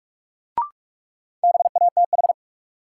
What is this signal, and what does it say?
Morse code, heard as a keyed tone, about 46 wpm, 700 Hz. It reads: BATH
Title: Sentences from Top 1600 Words - New Word Drill - 45wpm